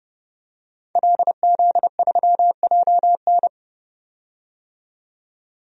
Morse code, heard as a keyed tone, about 30 wpm, 700 Hz. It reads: LZ3JN
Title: DXCC-100 Call Signs - 5 Characters - 30wpm